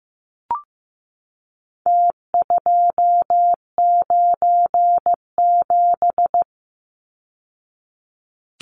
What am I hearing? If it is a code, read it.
Morse code, heard as a keyed tone, about 15 wpm, 700 Hz. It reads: T297
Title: Numbers - Contest Style - 4-Digit 15wpm